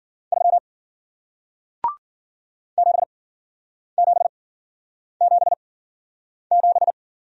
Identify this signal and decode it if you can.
Morse code, heard as a keyed tone, about 53 wpm, spaced as for 9 wpm, 700 Hz. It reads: 3 7777